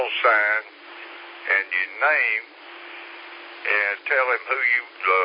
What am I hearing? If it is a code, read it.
Emergency Alert System (SAME) tones, heard as harsh buzzing bursts, with speech. It is heard from 0.3 to 4.2 s.